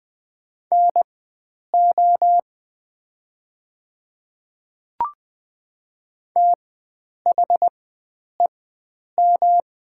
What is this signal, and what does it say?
Morse code, heard as a keyed tone, about 20 wpm, spaced as for 9 wpm, 700 Hz. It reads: NO THEM
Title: 4x Character Spacing - Top 100 Words - 20wpm